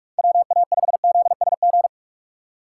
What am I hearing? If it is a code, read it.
Morse code, heard as a keyed tone, about 46 wpm, 700 Hz. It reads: WA5ZSG
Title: USA Call Signs - 3x Character Spacing - 45wpm